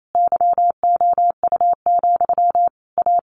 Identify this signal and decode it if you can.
Morse code, heard as a keyed tone, about 28 wpm, 700 Hz. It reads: YOU, A
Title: A Sound Of Thunder 28wpm